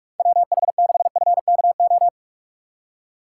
Morse code, heard as a keyed tone, about 44 wpm, 700 Hz. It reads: WH6FKO
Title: USA Call Signs - 6x Character Spacing - 45wpm